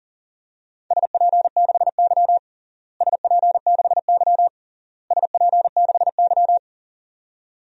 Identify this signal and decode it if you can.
Morse code, heard as a keyed tone, about 40 wpm, 700 Hz. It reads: SP6Y SP6Y SP6Y
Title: DXCC-100 Call Signs - 4 Characters - Repeated 3x - 40wpm